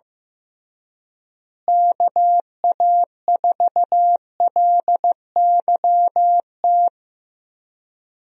Morse code, heard as a keyed tone, about 15 wpm, 700 Hz. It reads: KA4LYT
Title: USA Call Signs - 6 Characters - Repeated 3x - 15wpm